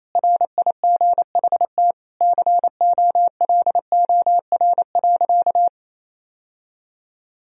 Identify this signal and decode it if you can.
Morse code, heard as a keyed tone, about 28 wpm, 700 Hz. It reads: RIGHT COLOR.
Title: Sentences from Top 1000 Words - Review Part 2 - 28wpm